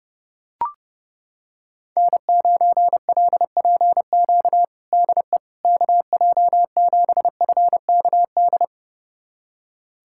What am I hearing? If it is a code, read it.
Morse code, heard as a keyed tone, about 30 wpm, 700 Hz. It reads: N9LPQ DE KJ7FKD
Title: USA Callsign DE Callsign 30wpm